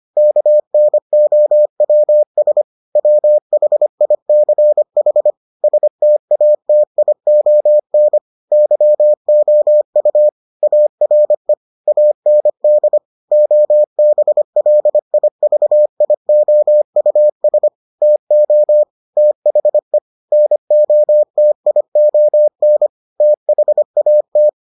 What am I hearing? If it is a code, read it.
Morse code, heard as a keyed tone, about 25 wpm, 600 Hz. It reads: KNOWS WHICH STATION YOU ARE AND OBLIVIOUS TO THE NOTION THAT